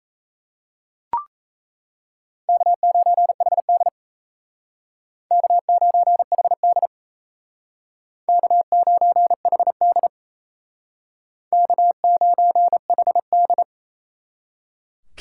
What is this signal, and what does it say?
Morse code, heard as a keyed tone, about 35 wpm, 700 Hz. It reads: K9HD K9HD K9HD K9HD
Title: Speed Racing - USA 1x2 Call Signs 28wpm